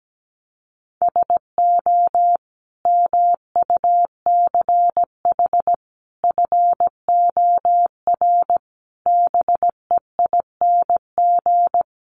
Morse code, heard as a keyed tone, about 17 wpm, 700 Hz. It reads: SO MUCH FOR BEING
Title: Sets of 5 Words Encore 17wpm